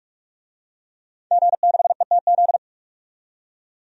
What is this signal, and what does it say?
Morse code, heard as a keyed tone, about 43 wpm, 700 Hz. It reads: G6ETZ
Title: DXCC-100 Call Signs - 5 Characters - 45wpm